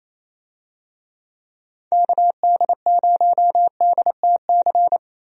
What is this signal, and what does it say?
Morse code, heard as a keyed tone, about 28 wpm, 700 Hz. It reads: KD0DTC
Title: USA Call Signs - 6 Characters - 28wpm